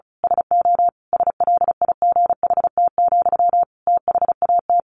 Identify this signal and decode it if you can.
Morse code, heard as a keyed tone, about 34 wpm, 700 Hz. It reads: SO SLIGHT, THAT
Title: A Sound Of Thunder 35wpm